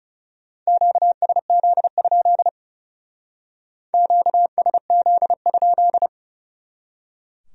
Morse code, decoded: QSZ? QSZ?

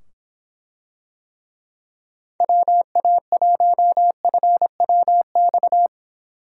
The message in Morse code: WA1FWX

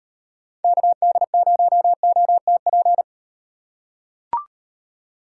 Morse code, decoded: KD0OTP

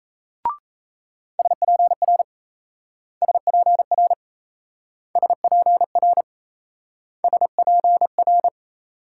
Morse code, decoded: SPR SPR SPR SPR

